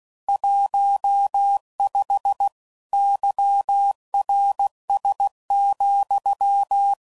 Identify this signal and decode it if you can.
Morse code, heard as a keyed tone, about 16 wpm, 800 Hz. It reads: 15 YRS,
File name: CW Code Groups @15WPM / QSOs / WD0BGP Spaced